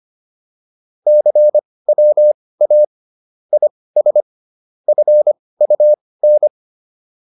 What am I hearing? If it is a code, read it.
Morse code, heard as a keyed tone, about 25 wpm, spaced as for 18 wpm, 600 Hz. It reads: CWA IS FUN